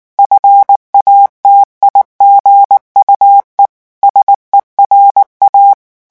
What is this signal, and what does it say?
Morse code, heard as a keyed tone, about 19 wpm, 800 Hz. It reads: FATIGUE SERA
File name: morse mp3 / revision9 citations humoristiques vitesse 18 mots minute